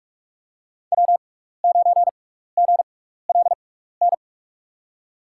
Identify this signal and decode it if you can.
Morse code, heard as a keyed tone, about 43 wpm, spaced as for 15 wpm, 700 Hz. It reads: W9GLN
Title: USA Call Signs - 5 Characters - 6x Character Spacing - 45wpm